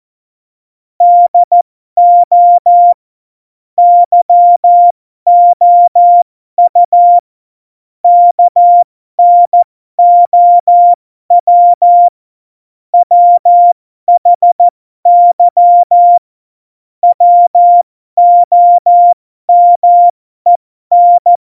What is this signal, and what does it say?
Morse code, heard as a keyed tone, about 14 wpm, spaced as for 12 wpm, 700 Hz. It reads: DO YOU KNOW WHY WOMEN
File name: Morsekurs / Witze